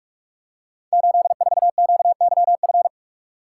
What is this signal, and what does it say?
Morse code, heard as a keyed tone, about 44 wpm, 700 Hz. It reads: 84QYF
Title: Code Groups - 5 Letter:Numbers - 45wpm